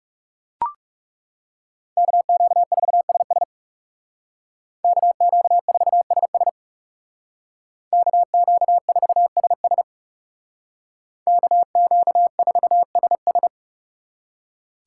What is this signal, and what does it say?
Morse code, heard as a keyed tone, about 39 wpm, 700 Hz. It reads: KQ4SS KQ4SS KQ4SS KQ4SS